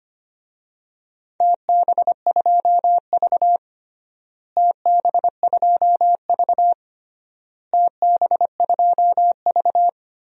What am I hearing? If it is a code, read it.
Morse code, heard as a keyed tone, about 25 wpm, 700 Hz. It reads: TB2V TB2V TB2V
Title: DXCC-100 Call Signs - 4 Characters - Repeated 3x - 25wpm